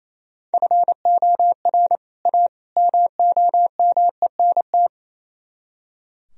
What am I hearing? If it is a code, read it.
Morse code, heard as a keyed tone, about 34 wpm, 700 Hz. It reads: FOR A MOMENT
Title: Speed-Racing - Sets of 3 Words 28wpm